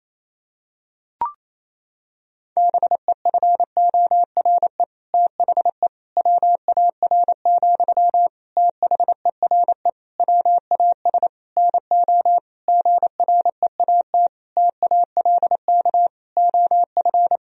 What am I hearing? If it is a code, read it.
Morse code, heard as a keyed tone, about 28 wpm, 700 Hz. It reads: BEFORE THE WAR, THERE WAS NO GREAT TALK OF
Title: Sentences from Top 1000 Words - Review Part 2 - 28wpm